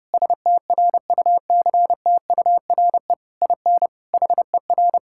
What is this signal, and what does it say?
Morse code, heard as a keyed tone, about 30 wpm, 700 Hz. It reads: STRUCTURE IN HER
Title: Sentences from Top 1000 Words - Review Part 6 - 30wpm